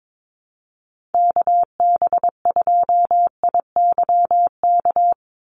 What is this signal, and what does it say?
Morse code, heard as a keyed tone, about 22 wpm, 700 Hz. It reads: KB2IYK
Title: USA Call Signs 22wpm